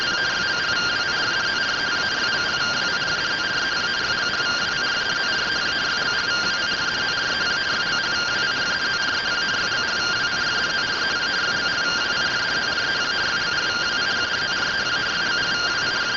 Two-tone teleprinter signal, about 170 Hz shift.